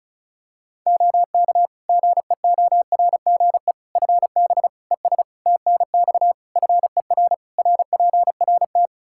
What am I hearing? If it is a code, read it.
Morse code, heard as a keyed tone, about 34 wpm, 700 Hz. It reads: OK GEORGE FB ES TNX FER RPRT
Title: K1USN SST QSO Practice 35wpm